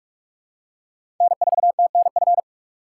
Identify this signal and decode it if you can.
Morse code, heard as a keyed tone, about 46 wpm, 700 Hz. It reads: N4TNF
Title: USA Call Signs - 5 Characters - 8x Character Spacing - 45wpm